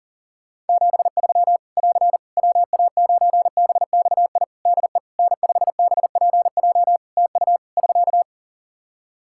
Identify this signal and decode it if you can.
Morse code, heard as a keyed tone, about 40 wpm, 700 Hz. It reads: 73 <AR> WA9BXI DE N5BPJ TU <SK>
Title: K1USN SST QSO Practice 40wpm